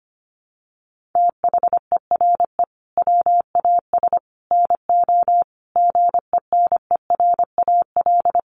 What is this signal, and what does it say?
Morse code, heard as a keyed tone, about 25 wpm, 700 Hz. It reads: THERE WAS NO GENERAL
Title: Sentences from Top 1200 Words 25wpm